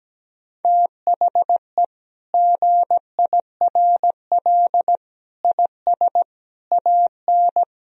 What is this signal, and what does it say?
Morse code, heard as a keyed tone, about 17 wpm, 700 Hz. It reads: THE GIRL IS AN
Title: Sentences from Top 1000 Words - Review Part 4 - 17wpm